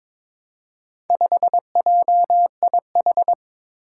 Morse code, heard as a keyed tone, about 22 wpm, 700 Hz. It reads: 5JIH
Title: Four Challenging Letter-Number Combinations 22wpm